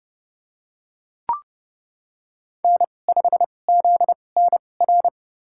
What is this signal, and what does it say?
Morse code, heard as a keyed tone, about 32 wpm, spaced as for 22 wpm, 700 Hz. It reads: N5ZNR
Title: USA Call Signs - 5 Characters - 2x Character Spacing - 30wpm